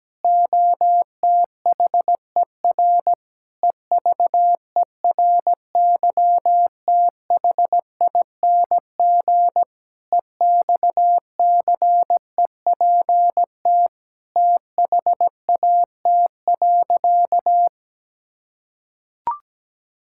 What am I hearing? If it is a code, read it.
Morse code, heard as a keyed tone, about 17 wpm, 700 Hz. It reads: OTHER EVERYTHING EXCEPT THAT.
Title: Sentences from Top 1600 Words 17wpm